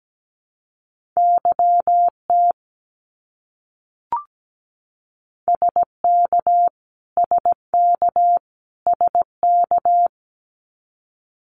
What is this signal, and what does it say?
Morse code, heard as a keyed tone, about 17 wpm, 700 Hz. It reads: YT SK SK SK
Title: Canadian Provinces and Territories - Repeated 3x - 17wpm